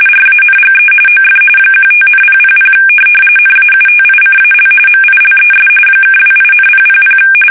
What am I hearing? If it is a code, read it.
RTTY, about 45.45 baud. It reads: RANGE OF 15,000 NAUTICAL MILES (28,000
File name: fsk850 / USS-Massachusetts